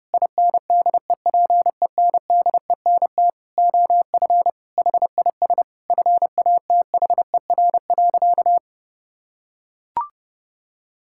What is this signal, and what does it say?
Morse code, heard as a keyed tone, about 30 wpm, 700 Hz. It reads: INDEPENDENT OF HIS FATHER.
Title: Sentences from Top 1000 Words - Review Part 6 - 30wpm